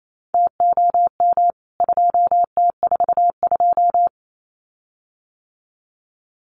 Morse code, heard as a keyed tone, about 28 wpm, 700 Hz. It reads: TOM 2T42
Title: CWT Exchange 28wpm